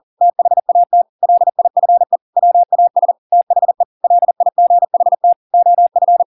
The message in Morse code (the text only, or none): THAT LIFE WAS THE LIGHT OF